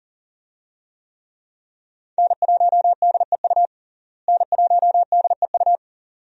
Morse code, heard as a keyed tone, about 40 wpm, 700 Hz. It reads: N1DEU N1DEU
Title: USA Call Signs - 5 Characters - Repeated 2x - 40wpm